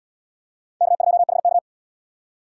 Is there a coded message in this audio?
Yes: Morse code from 0.8 to 1.6 s, heard as a keyed tone, about 73 wpm, 700 Hz. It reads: K2HK